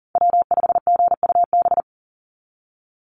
Morse code, heard as a keyed tone, about 40 wpm, 690 Hz. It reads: W5GUB
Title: USA Call Signs 40wpm